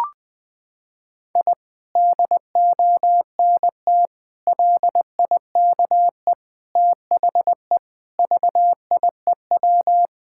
Morse code, heard as a keyed tone, about 21 wpm, 700 Hz. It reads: I DONT LIKE THE VIEW
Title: Sentences from Top 1000 Words - Review Part 6 - 20wpm